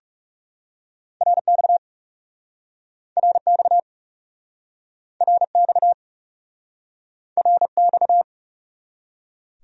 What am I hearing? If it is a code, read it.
Morse code, heard as a keyed tone, about 42 wpm, 700 Hz. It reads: RX RX RX RX